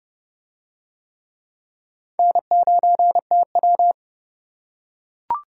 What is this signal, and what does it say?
Morse code, heard as a keyed tone, about 29 wpm, 700 Hz. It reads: N9TW